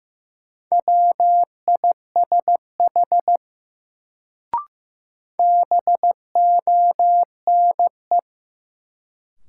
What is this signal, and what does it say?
Morse code, heard as a keyed tone, about 15 wpm, 700 Hz. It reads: WISH BONE
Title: Sentences from Top 1600 Words - New Word Drill - 15wpm